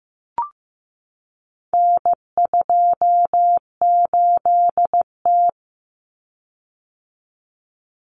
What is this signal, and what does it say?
Morse code, heard as a keyed tone, about 15 wpm, 700 Hz. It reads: N28T
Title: Common Cut Numbers - 4-Digit 15wpm